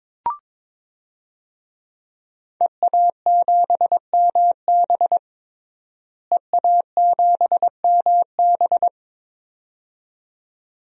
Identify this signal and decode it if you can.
Morse code, heard as a keyed tone, about 22 wpm, 700 Hz. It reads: EA7MB EA7MB
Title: DXCC-100 Call Signs - 5 Characters - Repeated 2x - 22wpm